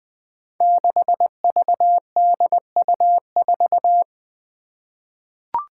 Morse code, heard as a keyed tone, about 20 wpm, 700 Hz. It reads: 6VDU4
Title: Five Challenging Letter-Number Combinations 20wpm